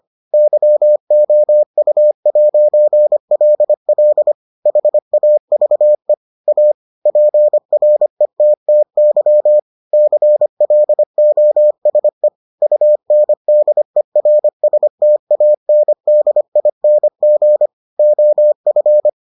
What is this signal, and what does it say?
Morse code, heard as a keyed tone, about 25 wpm, 600 Hz. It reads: YOU'LL HAVE A PRETTY CLOSE UNDERSTANDING OF